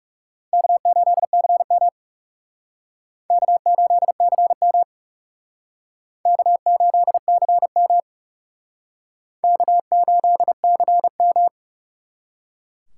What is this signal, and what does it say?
Morse code, heard as a keyed tone, about 37 wpm, 700 Hz. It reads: K8CM K8CM K8CM K8CM
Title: Speed Racing - USA 1x2 Call Signs 30wpm